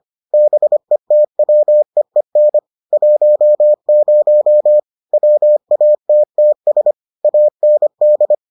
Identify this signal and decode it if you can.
Morse code, heard as a keyed tone, about 25 wpm, 600 Hz. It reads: BETWEEN 10 WATTS AND